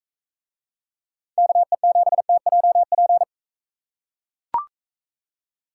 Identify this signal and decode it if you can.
Morse code, heard as a keyed tone, about 40 wpm, 700 Hz. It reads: KE7TJP